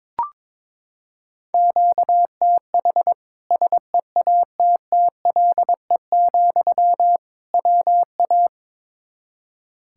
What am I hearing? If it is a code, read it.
Morse code, heard as a keyed tone, about 22 wpm, 700 Hz. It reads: QTH SEATTLE, WA
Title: Practice QSO - Exchange 1 - Beginning - 22wpm